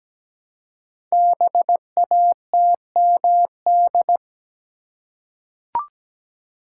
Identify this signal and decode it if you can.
Morse code, heard as a keyed tone, about 17 wpm, 700 Hz. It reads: BATMD
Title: Code Groups - 5 Letters - 17wpm